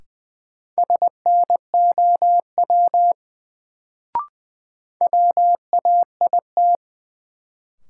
Morse code, heard as a keyed tone, about 20 wpm, 700 Hz. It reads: SNOW WAIT
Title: Sentences from Top 500 Words - New Word Drill - 20wpm